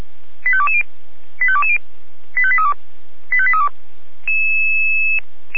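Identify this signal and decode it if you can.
Five-tone selective call (ZVEI), heard as quick runs of five tones, tones of about 70 ms. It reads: 85220 85220 85832 85832
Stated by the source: ZVEI five-tone call received by SDR.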